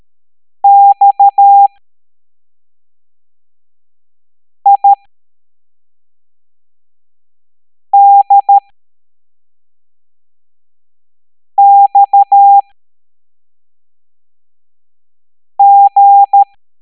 Morse code, heard as a keyed tone, about 13 wpm, spaced as for 3 wpm, 800 Hz. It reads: XIDXG